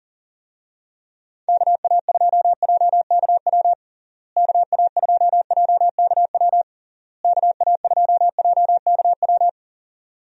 Morse code, heard as a keyed tone, about 40 wpm, 700 Hz. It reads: KA2JKW KA2JKW KA2JKW
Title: USA Call Signs - 6 Characters - Repeated 3x - 40wpm